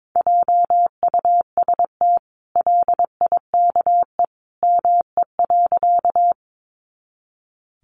Morse code, heard as a keyed tone, about 22 wpm, 700 Hz. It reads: JUST LIKE ME.
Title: Sentences from Top 200 Words 22wpm